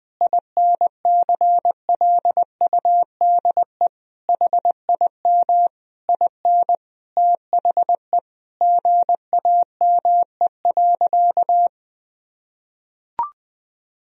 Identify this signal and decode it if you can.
Morse code, heard as a keyed tone, about 21 wpm, 700 Hz. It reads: INCLUDE HIM IN THE GAME.
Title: Sentences from Top 1000 Words - Review Part 3 - 20wpm